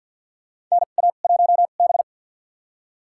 Morse code, heard as a keyed tone, about 52 wpm, spaced as for 37 wpm, 700 Hz. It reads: NA1B
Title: USA Call Signs - 4 Characters - 2x Character Spacing - 50wpm